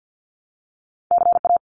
Morse code, heard as a keyed tone, about 49 wpm, 700 Hz. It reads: CA
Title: US States Repeated 3x 50wpm